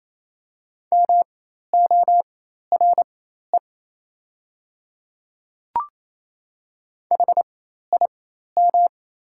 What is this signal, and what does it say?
Morse code, heard as a keyed tone, about 28 wpm, spaced as for 13 wpm, 700 Hz. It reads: MORE HIM
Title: 4x Character Spacing - Top 100 Words - 28wpm